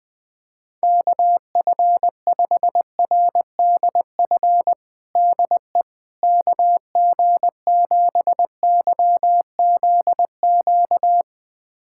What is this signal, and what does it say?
Morse code, heard as a keyed tone, about 20 wpm, 700 Hz. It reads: KF5RDF DE KG7YZQ
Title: USA Callsign DE Callsign - 6 Characters - 20wpm